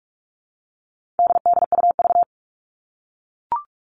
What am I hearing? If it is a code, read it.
Morse code, heard as a keyed tone, about 44 wpm, 700 Hz. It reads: DDUV
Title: All Cut Numbers - 4-Digit 45wpm